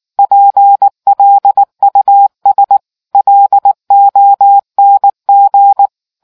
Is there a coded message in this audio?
Yes: Morse code from 0.2 to 5.9 s, heard as a keyed tone, about 19 wpm, 800 Hz. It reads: PLUS LONG